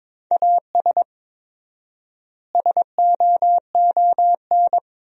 Morse code, heard as a keyed tone, about 22 wpm, 700 Hz. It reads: AS SOON
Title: 4x Word-Spacing - Sentences from Top 400 Words - 22wpm